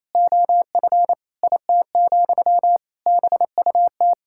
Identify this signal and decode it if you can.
Morse code, heard as a keyed tone, about 28 wpm, 700 Hz. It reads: OF IT, BUT
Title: Sentences from Top 1000 Words - Review Part 6 - 28wpm